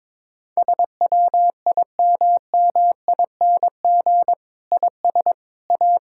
Morse code, heard as a keyed tone, about 21 wpm, 700 Hz. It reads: SWIMMING IS A
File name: Difficult Sentences with Hints - 22wpm